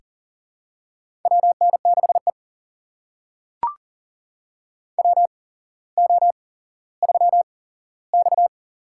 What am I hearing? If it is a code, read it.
Morse code, heard as a keyed tone, about 42 wpm, spaced as for 17 wpm, 700 Hz. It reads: WN6E WO3X
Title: USA Call Signs - 8x Character Spacing - 40wpm